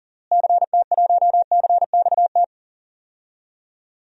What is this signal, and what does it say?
Morse code, heard as a keyed tone, about 37 wpm, 700 Hz. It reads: CT1CXT
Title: DXCC-100 Call Signs - 6 Characters - Repeated 3x - 40wpm